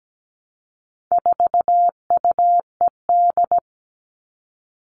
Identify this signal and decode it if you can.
Morse code, heard as a keyed tone, about 17 wpm, 700 Hz. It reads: 4UED
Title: All Cut Numbers - 4-Digit 17wpm